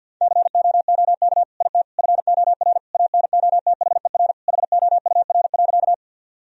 Morse code, heard as a keyed tone, about 50 wpm, 700 Hz. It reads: COOK IT FOR ANOTHER HOUR.
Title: Sentences from Top 1000 Words - Review Part 4 - 50wpm